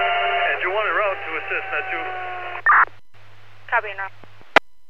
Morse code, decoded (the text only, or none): none